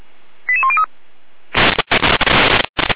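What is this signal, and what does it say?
Five-tone selective call (ZVEI), heard as quick runs of five tones, tones of about 70 ms. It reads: 88103
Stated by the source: Received via SDR as a ZVEI five-tone call, address 88102, on the Feuerwehr Orlamünde fire brigade audio server.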